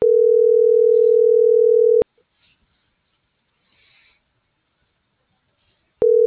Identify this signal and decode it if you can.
Ringback tone (North America), 440 + 480 Hz.